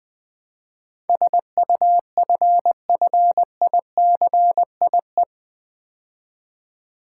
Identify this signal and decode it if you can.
Morse code, heard as a keyed tone, about 20 wpm, 700 Hz. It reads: SUFFICIE
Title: Build A Word - 4 Letters At A Time - 10-Letter Words - 20wpm